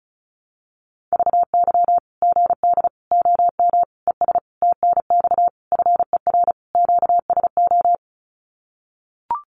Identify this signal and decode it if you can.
Morse code, heard as a keyed tone, about 34 wpm, 700 Hz. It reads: VY GD OM ES TNX FER QSO
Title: Common QSO Phrases 35wpm